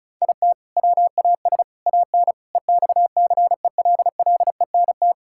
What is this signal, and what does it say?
Morse code, heard as a keyed tone, about 34 wpm, 700 Hz. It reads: IT WAS AN EXCELLENT
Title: Sentences from Top 1000 Words - Review Part 5 - 35wpm